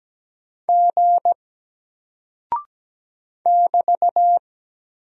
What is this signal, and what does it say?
Morse code, heard as a keyed tone, about 17 wpm, 700 Hz. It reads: G <BT>
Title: Single Character (Letter:Number:Punctuation:Prosign) 17wpm